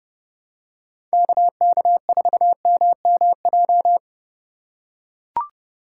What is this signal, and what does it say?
Morse code, heard as a keyed tone, about 32 wpm, 700 Hz. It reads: KK4MMJ
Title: USA Call Signs - 6 Characters - 2x Character Spacing - 30wpm